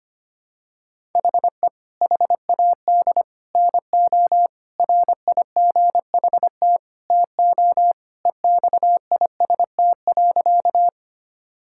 Morse code, heard as a keyed tone, about 25 wpm, 700 Hz. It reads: HE HAD NO RIGHT TO EXIST.
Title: Sentences from Top 1000 Words - Review Part 5 - 25wpm